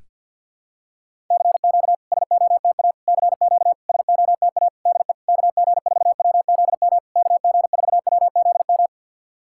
Morse code, heard as a keyed tone, about 50 wpm, 700 Hz. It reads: CQ SOTA CQ SOTA DE KG4WZM KG4WZM
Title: SOTA QSO Practice 50wpm